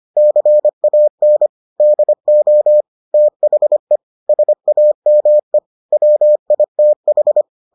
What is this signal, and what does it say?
Morse code, heard as a keyed tone, about 25 wpm, 600 Hz. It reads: CAN DO THE SAME WITH